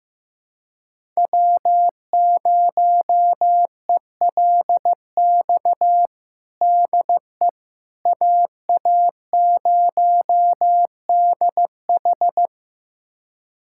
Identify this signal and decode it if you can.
Morse code, heard as a keyed tone, about 15 wpm, 700 Hz. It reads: W0ELX DE AA0DH
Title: USA Callsign DE Callsign - 5 Characters - 15wpm